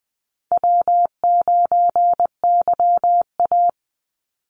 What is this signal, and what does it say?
Morse code, heard as a keyed tone, about 20 wpm, 700 Hz. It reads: W9YA
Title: Call Signs Repeated 2x - 20wpm